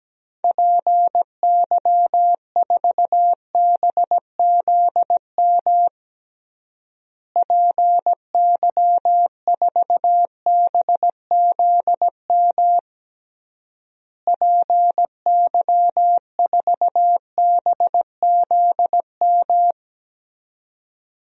Morse code, heard as a keyed tone, about 17 wpm, 700 Hz. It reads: PY4BZM PY4BZM PY4BZM
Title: DXCC-100 Call Signs - 6 Characters - Repeated 3x - 17wpm